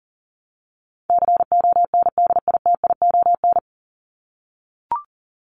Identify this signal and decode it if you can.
Morse code, heard as a keyed tone, about 38 wpm, 700 Hz. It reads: CONDITION
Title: Top 751-1000 Words 40wpm